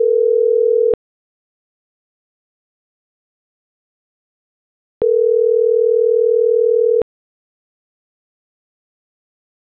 Ringback tone (North America), 440 + 480 Hz.